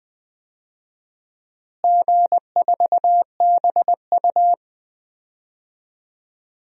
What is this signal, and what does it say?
Morse code, heard as a keyed tone, about 20 wpm, 700 Hz. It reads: G4BU
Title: DXCC-100 Call Signs - 4 Characters - Repeated 3x - 20wpm